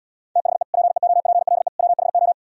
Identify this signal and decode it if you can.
Morse code, heard as a keyed tone, about 76 wpm, 700 Hz. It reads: THE PEOPLE WHO